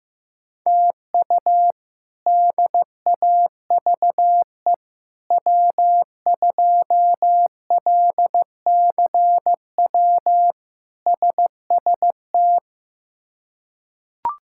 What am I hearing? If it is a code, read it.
Morse code, heard as a keyed tone, about 15 wpm, 700 Hz. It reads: TU DAVE W2LCW SST
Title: K1USN SST QSO Practice - 15wpm